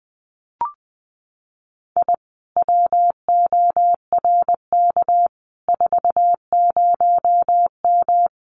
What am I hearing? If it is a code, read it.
Morse code, heard as a keyed tone, about 20 wpm, 700 Hz. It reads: I WORK 40M
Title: Common QSO Phrases Encore 20wpm